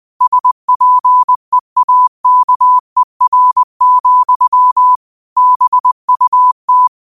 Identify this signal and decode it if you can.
Morse code, heard as a keyed tone, about 20 wpm, 1000 Hz. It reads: SPEAKER, BUT